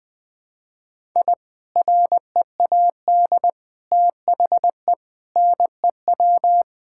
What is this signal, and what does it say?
Morse code, heard as a keyed tone, about 20 wpm, 700 Hz. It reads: I READ THE NEW
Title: Sentences with 3-Word Phrases - 20wpm